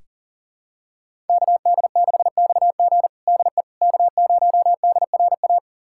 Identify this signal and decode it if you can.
Morse code, heard as a keyed tone, about 40 wpm, 700 Hz. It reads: KD6XG DE K0DRA
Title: USA Callsign DE Callsign - 5 Characters - 40wpm